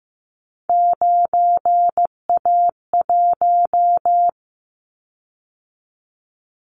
Morse code, heard as a keyed tone, about 15 wpm, 700 Hz. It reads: 9A1
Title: Callsign Prefixes for Top 50 DXCC - 15wpm